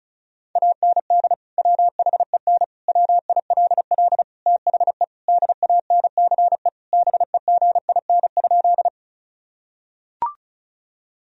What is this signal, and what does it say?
Morse code, heard as a keyed tone, about 36 wpm, 700 Hz. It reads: AND WHEN WILL THE DANCE BEGIN?
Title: Sentences from Top 1000 Words - Review Part 4 - 35wpm